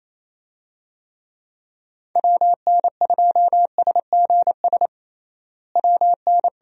Morse code, heard as a keyed tone, about 28 wpm, 700 Hz. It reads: WN2SGS WN
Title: USA Call Signs - 6 Characters - Repeated 3x - 28wpm